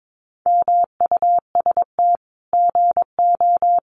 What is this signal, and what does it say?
Morse code, heard as a keyed tone, about 22 wpm, 700 Hz. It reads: MUST GO
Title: Common QSO Phrases 22wpm